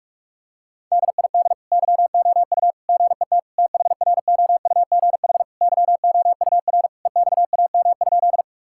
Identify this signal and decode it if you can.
Morse code, heard as a keyed tone, about 46 wpm, 690 Hz. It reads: DID YOU GET THROUGH YOUR EXAM?
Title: Sentences from Top 1000 Words - Review Part 5 - 45wpm